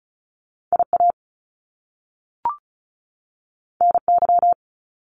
Morse code, heard as a keyed tone, about 36 wpm, 700 Hz. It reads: IA NY
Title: US States 35wpm